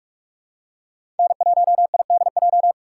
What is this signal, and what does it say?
Morse code, heard as a keyed tone, about 43 wpm, 700 Hz. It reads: N1IDJ
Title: USA Call Signs - 5 Characters - 6x Character Spacing - 45wpm